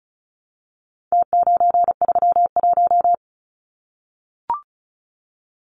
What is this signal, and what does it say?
Morse code, heard as a keyed tone, about 34 wpm, 700 Hz. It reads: T931
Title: Numbers - Contest Style - 4-Digit 35wpm